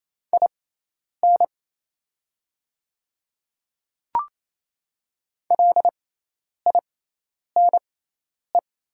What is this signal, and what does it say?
Morse code, heard as a keyed tone, about 27 wpm, spaced as for 10 wpm, 700 Hz. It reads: IN LINE
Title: 6x Character Spacing - Top 200 Words - 28wpm